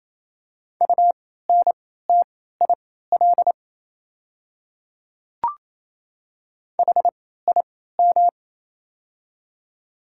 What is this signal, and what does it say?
Morse code, heard as a keyed tone, about 28 wpm, spaced as for 16 wpm, 700 Hz. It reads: UNTIL HIM
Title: 3x Character Spacing - Top 300 Words - 28wpm